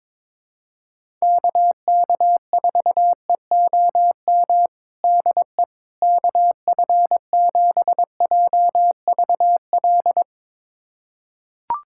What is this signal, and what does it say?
Morse code, heard as a keyed tone, about 22 wpm, 700 Hz. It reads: KK4EOM DE KF7JVL
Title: USA Callsign DE Callsign - 6 Characters - 22wpm